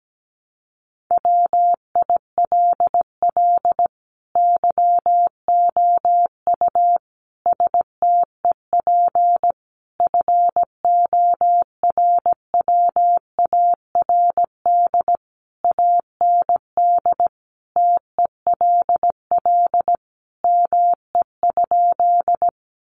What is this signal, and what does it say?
Morse code, heard as a keyed tone, about 17 wpm, 700 Hz. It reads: WILL YOU STEP FORWARD AND TELL ME?
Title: Sentences from Top 1300 Words 17wpm